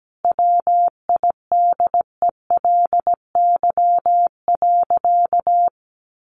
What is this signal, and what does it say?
Morse code, heard as a keyed tone, about 17 wpm, 700 Hz. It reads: WIDELY.
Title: Sentences from Top 1200 Words 17wpm